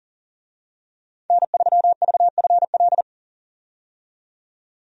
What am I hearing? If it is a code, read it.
Morse code, heard as a keyed tone, about 37 wpm, 700 Hz. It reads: N3VFL
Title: USA Call Signs - 5 Characters - 40wpm